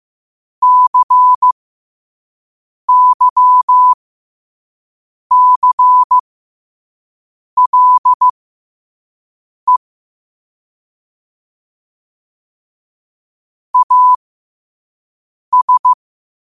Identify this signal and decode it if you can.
Morse code, heard as a keyed tone, about 15 wpm, spaced as for 5 wpm, 1000 Hz. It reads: CYCLE AS
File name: qotdfiles / 5wpm